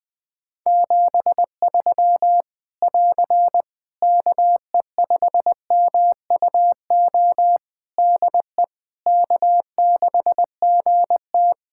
Morse code, heard as a keyed tone, about 20 wpm, 700 Hz. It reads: 73 <AR> KE5MUO DE K6GT
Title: Practice QSO - Exchange 4 - Ending - 20wpm